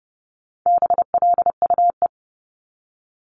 Morse code, heard as a keyed tone, about 30 wpm, 700 Hz. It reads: BLUE
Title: Multi-letter Phonemes 30wpm